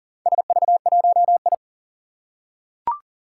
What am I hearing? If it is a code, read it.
Morse code, heard as a keyed tone, about 38 wpm, 700 Hz. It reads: SV1I
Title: Four Challenging Letter-Number Combinations 40wpm